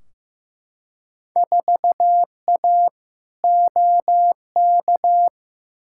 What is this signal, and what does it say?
Morse code, heard as a keyed tone, about 15 wpm, 700 Hz. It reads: 4A OK
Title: Field Day Exchanges - 15wpm